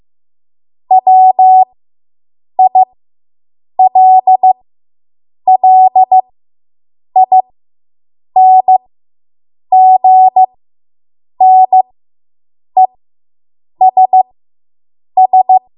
Morse code, heard as a keyed tone, about 15 wpm, spaced as for 7 wpm, 750 Hz. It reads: WILLINGNESS